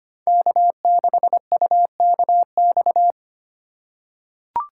Morse code, heard as a keyed tone, about 25 wpm, 700 Hz. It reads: K6UKX
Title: USA Call Signs - 5 Characters - 4x Character Spacing - 25wpm